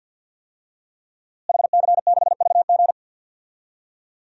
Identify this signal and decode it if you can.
Morse code, heard as a keyed tone, about 46 wpm, 700 Hz. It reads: HC6VG
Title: DXCC-100 Call Signs - 5 Characters - Repeated 3x - 50wpm